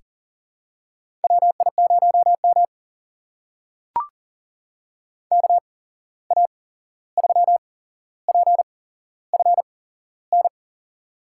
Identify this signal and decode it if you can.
Morse code, heard as a keyed tone, about 40 wpm, spaced as for 15 wpm, 700 Hz. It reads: WI0M KA3PFN
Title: USA Call Signs - 8x Character Spacing - 40wpm